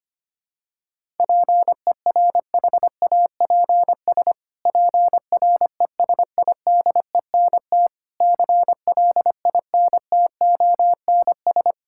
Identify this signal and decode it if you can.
Morse code, heard as a keyed tone, about 25 wpm, 700 Hz. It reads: PERHAPS PRESIDENT CLINTONS
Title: Pangram Challenge v2 - 25wpm